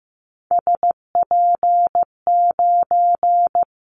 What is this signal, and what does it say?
Morse code, heard as a keyed tone, about 15 wpm, 700 Hz. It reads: SP9
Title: Callsign Prefixes for Top 25 DXCC - Repeated 3x - 15wpm